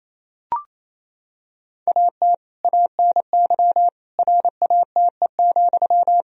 Morse code, heard as a keyed tone, about 28 wpm, 700 Hz. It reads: AT ANY RATE,
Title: Sentences from Top 1000 Words - Review Part 4 - 28wpm